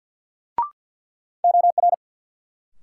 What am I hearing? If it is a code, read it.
Morse code, heard as a keyed tone, about 53 wpm, 700 Hz. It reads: OR
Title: Sentences from Top 100 Words - New Word Drill - 50wpm